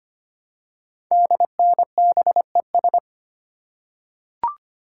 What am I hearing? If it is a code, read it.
Morse code, heard as a keyed tone, about 25 wpm, 700 Hz. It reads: DNBES
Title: Code Groups - 5 Letters - 25wpm